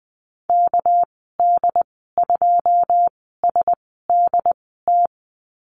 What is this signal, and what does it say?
Morse code, heard as a keyed tone, about 20 wpm, spaced as for 15 wpm, 700 Hz. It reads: KD2SDT
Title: 200 Lesson 32g - Letter k - Call Signs 20wpm